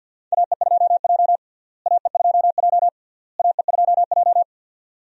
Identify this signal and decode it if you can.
Morse code, heard as a keyed tone, about 51 wpm, 700 Hz. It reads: AE2J AE2J AE2J
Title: USA Call Signs - 4 Characters - Repeated 3x - 50wpm